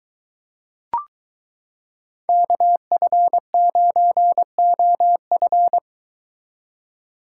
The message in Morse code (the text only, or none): KF9OF